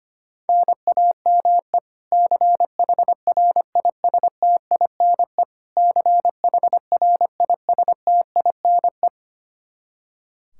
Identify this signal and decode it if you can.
Morse code, heard as a keyed tone, about 25 wpm, 700 Hz. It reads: NAME CHRISTINE CHRISTINE
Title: Practice QSO - Exchange 1 - Beginning - 25wpm